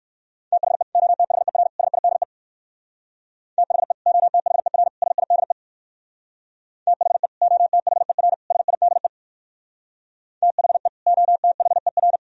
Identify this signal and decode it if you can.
Morse code, heard as a keyed tone, about 64 wpm, 700 Hz. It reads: THE OTHER SIDE THE OTHER SIDE THE OTHER SIDE THE OTHER